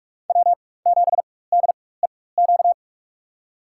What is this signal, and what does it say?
Morse code, heard as a keyed tone, about 44 wpm, spaced as for 21 wpm, 700 Hz. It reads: W7DEQ
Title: USA Call Signs - 5 Characters - 4x Character Spacing - 45wpm